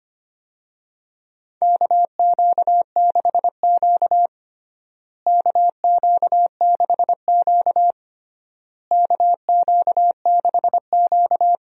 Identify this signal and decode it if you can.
Morse code, heard as a keyed tone, about 25 wpm, 700 Hz. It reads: KQ6Q KQ6Q KQ6Q
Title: USA Call Signs - 4 Characters - Repeated 3x - 25wpm